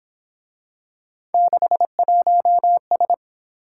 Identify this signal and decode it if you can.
Morse code, heard as a keyed tone, about 26 wpm, 700 Hz. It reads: 61S
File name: Speed-Racing - Three Letter-Number Combinations - 22wpm